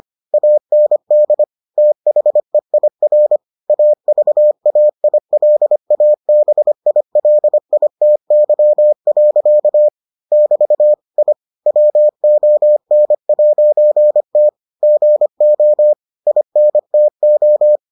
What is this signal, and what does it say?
Morse code, heard as a keyed tone, about 25 wpm, 600 Hz. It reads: AND THEIR AVAILABILITY. <BT> I WON'T GO INTO